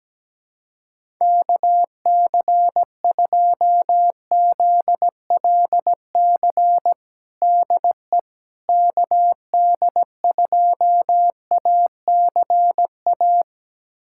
Morse code, heard as a keyed tone, about 17 wpm, 700 Hz. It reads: KC2ZLC DE KD2ACA
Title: USA Callsign DE Callsign 17wpm